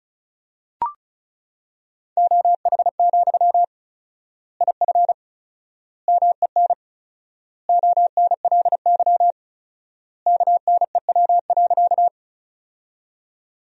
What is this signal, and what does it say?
Morse code, heard as a keyed tone, about 36 wpm, 700 Hz. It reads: OH, IF MEN ONLY KNEW.
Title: 4x Word-Spacing - Sentences from Top 500 Words - 35wpm